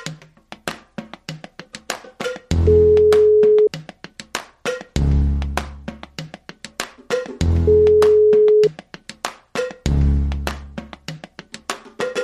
Ringback tone (Europe, 425 Hz).